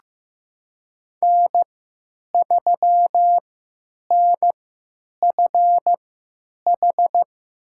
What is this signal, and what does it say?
Morse code, heard as a keyed tone, about 15 wpm, spaced as for 9 wpm, 700 Hz. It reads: N3NFH
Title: USA Call Signs - 5 Characters - 3x Character Spacing - 15wpm